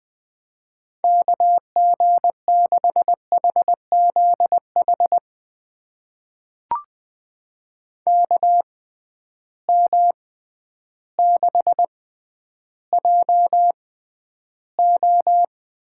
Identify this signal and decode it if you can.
Morse code, heard as a keyed tone, about 20 wpm, spaced as for 11 wpm, 700 Hz. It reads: KG6HZH KM6JO
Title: USA Call Signs - 6 Characters - 6x Character Spacing - 20wpm